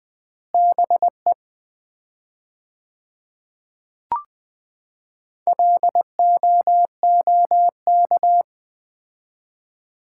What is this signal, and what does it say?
Morse code, heard as a keyed tone, about 20 wpm, 700 Hz. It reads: BE LOOK
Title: 4x Word-Spacing - Sets of 3 Words - 20wpm